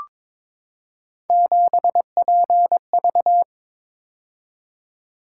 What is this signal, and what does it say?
Morse code, heard as a keyed tone, about 22 wpm, 700 Hz. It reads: 7PV
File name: practice / 22wpm / Three Letter-Number Combinations - 22wpm